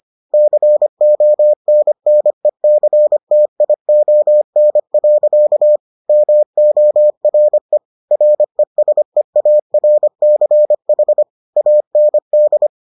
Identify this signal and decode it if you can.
Morse code, heard as a keyed tone, about 25 wpm, 600 Hz. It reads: CONNECTION. MORE RESEARCH AND